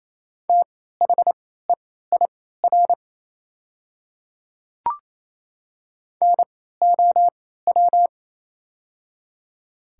Morse code, heard as a keyed tone, about 28 wpm, spaced as for 16 wpm, 700 Hz. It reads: THEIR NOW
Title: 3x Character Spacing - Top 100 Words - 28wpm